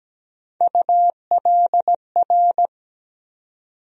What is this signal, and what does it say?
Morse code, heard as a keyed tone, about 17 wpm, 700 Hz. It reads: ULR
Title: USA Call Sign Postfixes - 3 Letters 17wpm